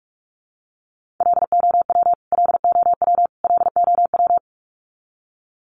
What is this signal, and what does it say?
Morse code, heard as a keyed tone, about 44 wpm, 700 Hz. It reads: LOW LOW LOW
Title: Top 200 Words Repeated 45wpm